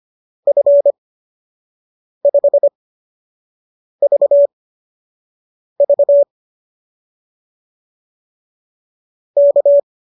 Morse code, heard as a keyed tone, about 25 wpm, spaced as for 6 wpm, 570 Hz. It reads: F5VV K